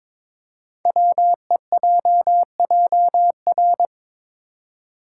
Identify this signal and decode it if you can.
Morse code, heard as a keyed tone, about 22 wpm, 700 Hz. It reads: WEJJR
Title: Code Groups - 5 Letters - 22wpm